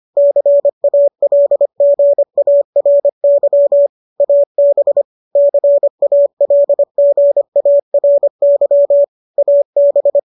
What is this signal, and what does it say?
Morse code, heard as a keyed tone, about 25 wpm, 570 Hz. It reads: CALGARY AB CALGARY AB
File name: qso203 25